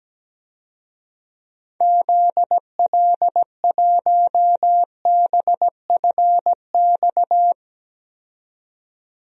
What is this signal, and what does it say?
Morse code, heard as a keyed tone, about 17 wpm, 700 Hz. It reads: ZL1BFX
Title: DXCC-100 Call Signs - 6 Characters - 17wpm